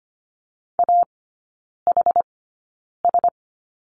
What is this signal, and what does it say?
Morse code, heard as a keyed tone, about 25 wpm, spaced as for 9 wpm, 700 Hz. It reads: AHS
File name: Lesson 14e - Number 2 - Three Characters 25wpm